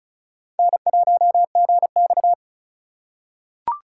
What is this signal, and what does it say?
Morse code, heard as a keyed tone, about 36 wpm, 700 Hz. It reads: N1GX
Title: USA Call Signs - 4 Characters - 2x Character Spacing - 35wpm